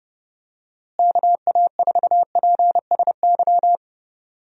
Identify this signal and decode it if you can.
Morse code, heard as a keyed tone, about 32 wpm, 700 Hz. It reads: KA4PSY
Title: USA Call Signs - 6 Characters - Repeated 2x - 30wpm